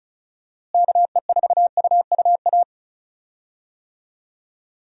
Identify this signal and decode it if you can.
Morse code, heard as a keyed tone, about 36 wpm, 700 Hz. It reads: KE4UUA
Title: USA Call Signs - 6 Characters - Repeated 2x - 35wpm